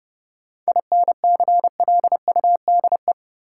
Morse code, heard as a keyed tone, about 31 wpm, 700 Hz. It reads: INCLUDE
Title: Sentences from Top 1000 Words - Review Part 3 - New Word Drill - 30wpm